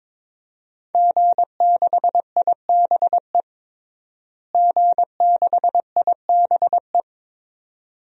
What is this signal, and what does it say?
Morse code, heard as a keyed tone, about 22 wpm, 700 Hz. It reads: G6IBE G6IBE
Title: DXCC-100 Call Signs - 5 Characters - Repeated 2x - 22wpm